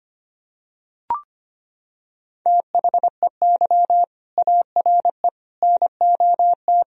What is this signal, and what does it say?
Morse code, heard as a keyed tone, about 25 wpm, 700 Hz. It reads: THEY ARE NOT
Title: Sentences from Top 1000 Words - Review Part 3 - 25wpm